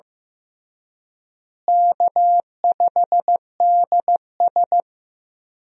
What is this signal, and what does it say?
Morse code, heard as a keyed tone, about 15 wpm, 700 Hz. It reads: K5DS
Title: USA Call Signs - 4 Characters - Repeated 3x - 15wpm